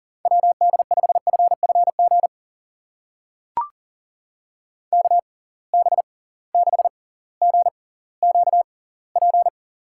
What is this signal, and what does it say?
Morse code, heard as a keyed tone, about 42 wpm, spaced as for 21 wpm, 700 Hz. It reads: WD5FFG KB6GQP
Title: USA Call Signs - 6 Characters - 6x Character Spacing - 40wpm